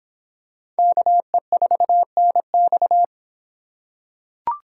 Morse code, heard as a keyed tone, about 26 wpm, 700 Hz. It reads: KE4NX